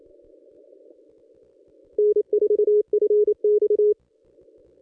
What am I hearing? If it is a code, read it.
Morse code, heard as a keyed tone, about 28 wpm, 420 Hz. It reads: N4FX
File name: contests / ss-cw-k5tr-op-n5rz -1200